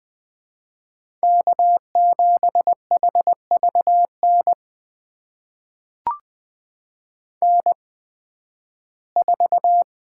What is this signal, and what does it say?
Morse code, heard as a keyed tone, about 21 wpm, spaced as for 13 wpm, 700 Hz. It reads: K7HVN N4
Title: USA Call Signs - 5 Characters - 8x Character Spacing - 20wpm